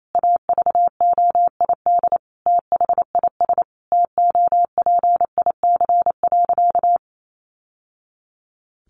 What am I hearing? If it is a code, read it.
Morse code, heard as a keyed tone, about 28 wpm, 700 Hz. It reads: AVOID THIS TOPIC.
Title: Sentences from Top 1200 Words 28wpm